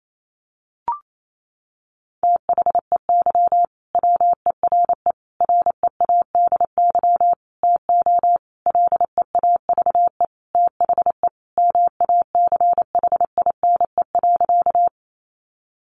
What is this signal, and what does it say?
Morse code, heard as a keyed tone, about 28 wpm, 700 Hz. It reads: THEY WERE READY TO LEAVE THE MACHINE.
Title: A Sound Of Thunder 28wpm